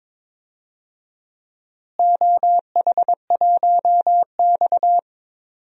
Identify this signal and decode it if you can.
Morse code, heard as a keyed tone, about 22 wpm, 700 Hz. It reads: OH1X
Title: DXCC-100 Call Signs - 4 Characters - Repeated 3x - 22wpm